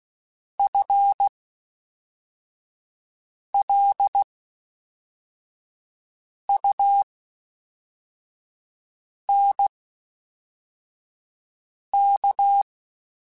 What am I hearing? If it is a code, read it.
Morse code, heard as a keyed tone, about 16 wpm, spaced as for 4 wpm, 800 Hz. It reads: FLUNK
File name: Spaced / FL Runs sp mp3s / FL Runs sp @5